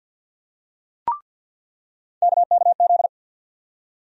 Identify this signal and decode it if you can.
Morse code, heard as a keyed tone, about 50 wpm, 700 Hz. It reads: KKZ
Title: USA Call Sign Postfixes - 3 Letters 50wpm